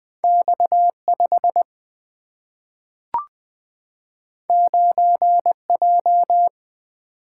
Morse code, heard as a keyed tone, about 20 wpm, 700 Hz. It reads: X 5 9 J
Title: Single-Letter Single-Number 20wpm